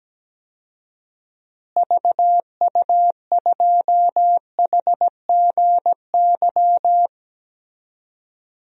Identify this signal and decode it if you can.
Morse code, heard as a keyed tone, about 17 wpm, 700 Hz. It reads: VU2HGY
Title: DXCC-100 Call Signs - 6 Characters - Repeated 3x - 17wpm